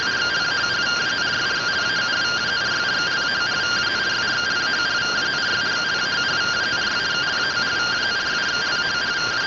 Two-tone teleprinter signal, about 170 Hz shift.